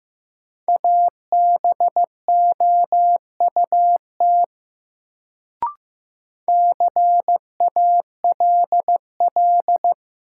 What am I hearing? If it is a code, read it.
Morse code, heard as a keyed tone, about 15 wpm, 700 Hz. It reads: ABOUT CALL
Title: Sentences from Top 100 Words - New Word Drill - 15wpm